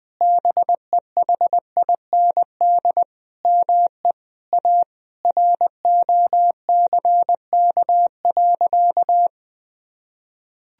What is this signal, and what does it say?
Morse code, heard as a keyed tone, about 20 wpm, 700 Hz. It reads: BEHIND ME A ROCK.
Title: Sentences from Top 1000 Words - Review Part 2 - 20wpm